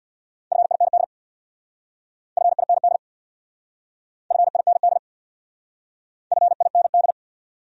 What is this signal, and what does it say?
Morse code, heard as a keyed tone, about 62 wpm, 700 Hz. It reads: FIND FIND FIND FIND
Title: Speed-Racing Top 500 Words 50wpm